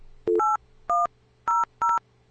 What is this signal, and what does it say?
DTMF, heard as telephone touch tones, tones of about 180 ms. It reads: 8100